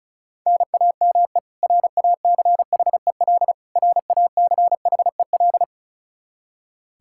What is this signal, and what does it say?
Morse code, heard as a keyed tone, about 36 wpm, 700 Hz. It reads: NAME RACHEL RACHEL
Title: Practice QSO - Exchange 1 - Beginning - 35wpm